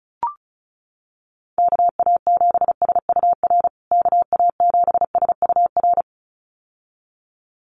Morse code, heard as a keyed tone, about 34 wpm, 700 Hz. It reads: KA7SUR KA7SUR
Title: USA Call Signs Repeated 2x 35wpm